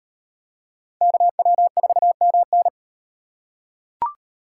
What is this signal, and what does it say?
Morse code, heard as a keyed tone, about 34 wpm, 700 Hz. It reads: KW4MN